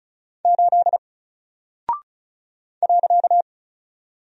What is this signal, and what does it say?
Morse code, heard as a keyed tone, about 36 wpm, 700 Hz. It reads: 8 .
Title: Single Character (Letter:Number:Punctuation:Prosign) 35wpm